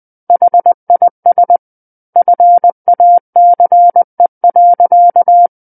Morse code, heard as a keyed tone, about 20 wpm, 700 Hz. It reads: HIS FACE.
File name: 20 WPM